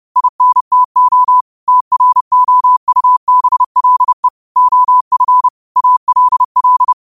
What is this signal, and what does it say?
Morse code, heard as a keyed tone, about 30 wpm, 1000 Hz. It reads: INTO TROUBLE OF ALL
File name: qotdfiles / 30wpm